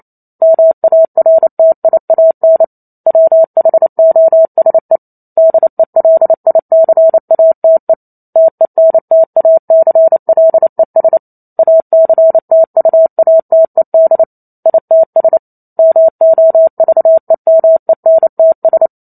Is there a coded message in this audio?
Yes: Morse code from 0.4 to 18.9 s, heard as a keyed tone, about 28 wpm, 650 Hz. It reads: MARTIAN WHOSE DELICATE TENTACLES ACTUATED ITS MOVEMENTS